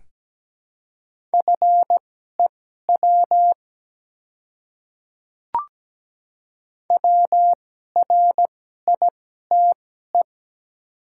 Morse code, heard as a keyed tone, about 17 wpm, spaced as for 12 wpm, 690 Hz. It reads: FEW WRITE
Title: 2x Character Spacing - Top 300 Words - 17wpm